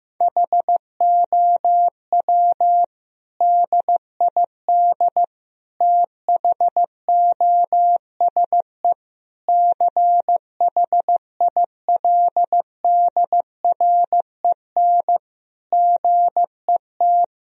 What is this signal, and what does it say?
Morse code, heard as a keyed tone, about 15 wpm, 700 Hz. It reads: HOW DID THOSE CHILDREN GET
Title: Sentences from Top 1000 Words - Review Part 1 - 15wpm